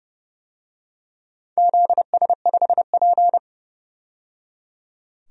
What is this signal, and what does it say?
Morse code, heard as a keyed tone, about 30 wpm, 700 Hz. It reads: ZS5P
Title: DXCC-100 Call Signs - 4 Characters - 30wpm